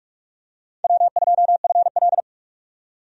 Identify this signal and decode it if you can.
Morse code, heard as a keyed tone, about 41 wpm, 700 Hz. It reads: W2FL
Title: USA Call Signs - 4 Characters - 45wpm